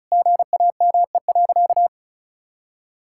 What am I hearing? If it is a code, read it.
Morse code, heard as a keyed tone, about 34 wpm, 700 Hz. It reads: GAME.
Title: Sentences from Top 1600 Words 35wpm